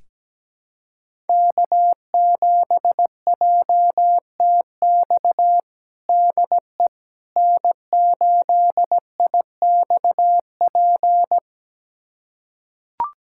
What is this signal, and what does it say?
Morse code, heard as a keyed tone, about 17 wpm, 700 Hz. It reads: K7JTX DE N8IXP
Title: USA Callsign DE Callsign - 5 Characters - 17wpm